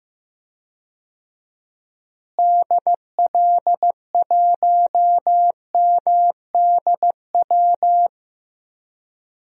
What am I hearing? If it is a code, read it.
Morse code, heard as a keyed tone, about 15 wpm, 700 Hz. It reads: DL1MDW
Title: DXCC-100 Call Signs - 6 Characters - Repeated 2x - 15wpm